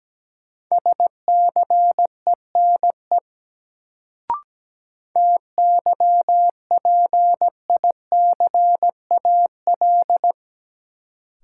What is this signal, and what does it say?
Morse code, heard as a keyed tone, about 17 wpm, 700 Hz. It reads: SCENE TYPICAL
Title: Sentences from Top 1100 Words - New Word Drill - 17wpm